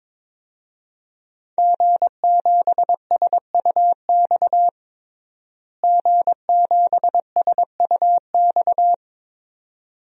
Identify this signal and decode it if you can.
Morse code, heard as a keyed tone, about 22 wpm, 700 Hz. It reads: G7SUX G7SUX
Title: DXCC-100 Call Signs - 5 Characters - Repeated 2x - 22wpm